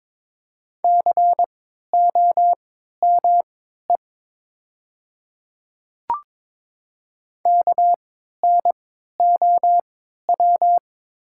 Morse code, heard as a keyed tone, about 21 wpm, spaced as for 12 wpm, 700 Hz. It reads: COME KNOW
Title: 3x Character Spacing - Top 100 Words - 22wpm